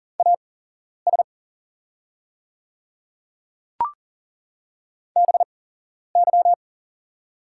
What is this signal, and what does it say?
Morse code, heard as a keyed tone, about 40 wpm, spaced as for 11 wpm, 700 Hz. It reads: AS BY
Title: 8x Character Spacing - Top 100 Words - 40wpm